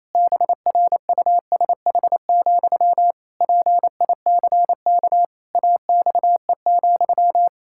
Morse code, heard as a keyed tone, about 27 wpm, 700 Hz. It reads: BRUSH, PICK AXE,
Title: Pangram Challenge v2 - 28wpm